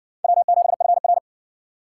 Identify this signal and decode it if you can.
Morse code, heard as a keyed tone, about 60 wpm, 700 Hz. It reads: W7FR